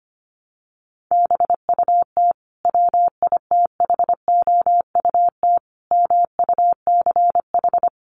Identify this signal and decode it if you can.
Morse code, heard as a keyed tone, about 25 wpm, 700 Hz. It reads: BUT WITHOUT MUCH
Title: Sentences from Top 900 Words 25wpm